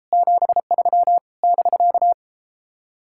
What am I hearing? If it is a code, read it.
Morse code, heard as a keyed tone, about 33 wpm, 700 Hz. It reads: 73 <BK>